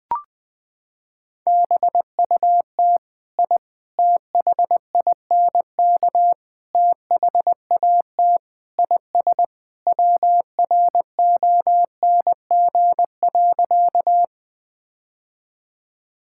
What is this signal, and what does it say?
Morse code, heard as a keyed tone, about 19 wpm, 700 Hz. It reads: BUT I THINK THAT IS WRONG.
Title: Sentences with 4-Word Phrases - 20wpm